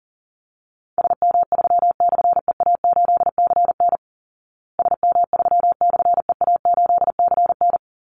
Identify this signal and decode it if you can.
Morse code, heard as a keyed tone, about 37 wpm, 700 Hz. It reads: SM3/EA8CN SM3/EA8CN
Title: Call Signs Repeated 2x - 40wpm